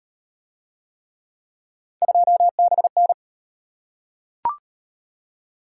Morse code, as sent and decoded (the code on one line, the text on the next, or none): ..--- -... -.
2BN